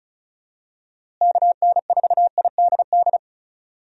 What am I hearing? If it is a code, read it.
Morse code, heard as a keyed tone, about 34 wpm, 700 Hz. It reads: KN4IDD
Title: USA Call Signs - 6 Characters - Repeated 3x - 35wpm